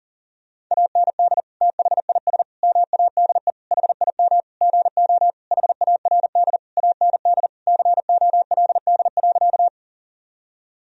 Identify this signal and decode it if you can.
Morse code, heard as a keyed tone, about 40 wpm, 700 Hz. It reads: AND THIS MADE HIM GO HARD AND COLD.
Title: Sentences from Top 1000 Words - Review Part 1 - 40wpm